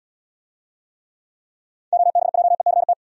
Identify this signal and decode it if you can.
Morse code, heard as a keyed tone, about 68 wpm, 700 Hz. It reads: OBJECT